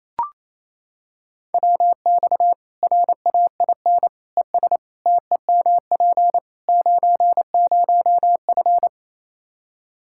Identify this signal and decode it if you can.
Morse code, heard as a keyed tone, about 28 wpm, 700 Hz. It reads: WX RAIN ES TEMP 90F
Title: Practice QSO - Exchange 2 - Info - 28wpm